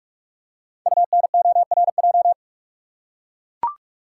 Morse code, heard as a keyed tone, about 44 wpm, 690 Hz. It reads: UNORJ
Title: Code Groups - 5 Letters - 45wpm